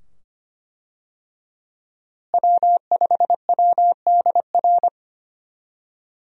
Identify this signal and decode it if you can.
Morse code, heard as a keyed tone, about 25 wpm, 700 Hz. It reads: W5WDR
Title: USA Call Signs - 5 Characters - Repeated 2x - 25wpm